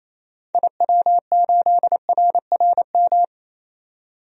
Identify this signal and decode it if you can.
Morse code, heard as a keyed tone, about 28 wpm, 700 Hz. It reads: IW8RRM
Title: DXCC-100 Call Signs - 6 Characters - 28wpm